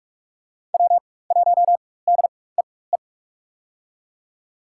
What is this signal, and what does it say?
Morse code, heard as a keyed tone, about 42 wpm, spaced as for 21 wpm, 700 Hz. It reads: W1DEE
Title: USA Call Signs - 5 Characters - 4x Character Spacing - 45wpm